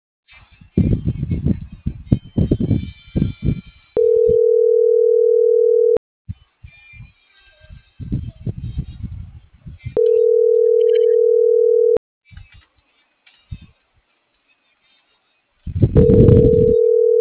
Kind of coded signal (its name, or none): telephone call progress tone